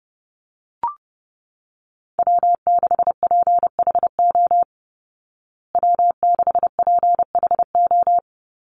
Morse code, heard as a keyed tone, about 30 wpm, 700 Hz. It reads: W6PHO W6PHO
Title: Call Signs Repeated 3x - 30wpm